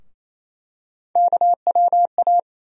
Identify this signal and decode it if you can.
Morse code, heard as a keyed tone, about 28 wpm, 700 Hz. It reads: KWA
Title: USA Call Sign Postfixes - 3 Letters 28wpm